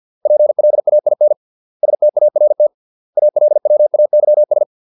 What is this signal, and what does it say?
Morse code, heard as a keyed tone, about 50 wpm, 600 Hz. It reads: PLAIN START ALWAYS